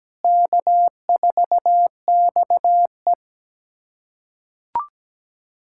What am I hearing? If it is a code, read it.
Morse code, heard as a keyed tone, about 17 wpm, 700 Hz. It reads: K4XE
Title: USA Callsign DE Callsign 17wpm